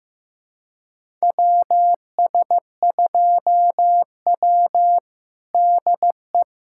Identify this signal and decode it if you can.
Morse code, heard as a keyed tone, about 15 wpm, 700 Hz. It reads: WS2W DE
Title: USA Callsign DE Callsign - 4 Characters - 15wpm